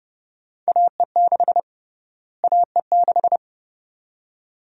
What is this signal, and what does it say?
Morse code, heard as a keyed tone, about 30 wpm, 700 Hz. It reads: AE6 AE6
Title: USA Call Sign Prefixes - Repeated 2x - 30wpm